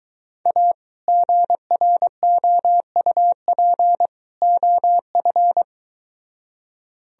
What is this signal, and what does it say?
Morse code, heard as a keyed tone, about 23 wpm, 700 Hz. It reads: A GROUP OF